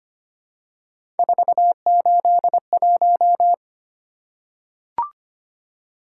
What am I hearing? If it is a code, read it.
Morse code, heard as a keyed tone, about 25 wpm, 700 Hz. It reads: 481
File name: Three Letter-Number Combinations - 25wpm